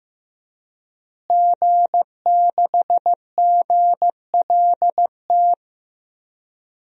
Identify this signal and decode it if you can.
Morse code, heard as a keyed tone, about 15 wpm, 700 Hz. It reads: G6GLT
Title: DXCC-100 Call Signs - 5 Characters - Repeated 2x - 15wpm